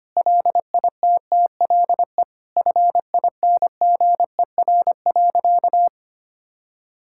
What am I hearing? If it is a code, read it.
Morse code, heard as a keyed tone, about 25 wpm, 700 Hz. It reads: LITTLE FINGER.
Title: Sentences from Top 1600 Words 25wpm